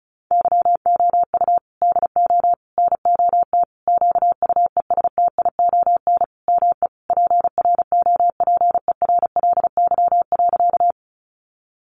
Morse code, heard as a keyed tone, about 34 wpm, 700 Hz. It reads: YOU DO NOT QUESTION ME PROPERLY.
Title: Sentences from Top 1100 Words 35wpm